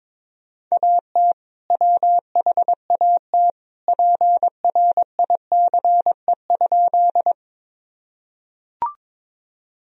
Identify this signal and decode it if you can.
Morse code, heard as a keyed tone, about 22 wpm, 700 Hz. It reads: AT WHAT PRICE?
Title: Sentences from Top 1000 Words - Review Part 4 - 22wpm